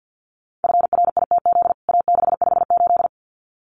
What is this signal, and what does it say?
Morse code, heard as a keyed tone, about 50 wpm, 700 Hz. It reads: FRITZ A658
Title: CWT Exchange 50wpm